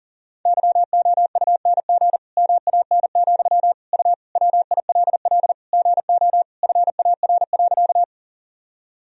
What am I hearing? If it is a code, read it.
Morse code, heard as a keyed tone, about 40 wpm, 700 Hz. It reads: YOUNG MAN, U WILL GO FAR.
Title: Sentences from Top 400 Words and Abbreviations 40wpm